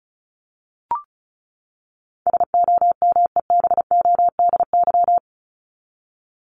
Morse code, heard as a keyed tone, about 34 wpm, 700 Hz. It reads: SOMEBODY
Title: Top 1751-2000 Words 35wpm